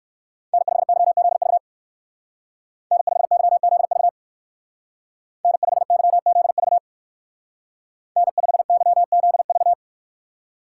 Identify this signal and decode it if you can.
Morse code, heard as a keyed tone, about 54 wpm, 700 Hz. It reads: N5YZV N5YZV N5YZV N5YZV
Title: Speed Racing - USA 1x3 Call Signs 45wpm